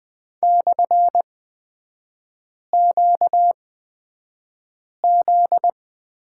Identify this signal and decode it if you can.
Morse code, heard as a keyed tone, about 20 wpm, spaced as for 5 wpm, 700 Hz. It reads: /QZ
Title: 243 Lesson 39e - Letter z - Three Characters 20wpm